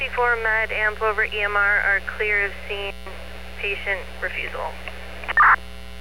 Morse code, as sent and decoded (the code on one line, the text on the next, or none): none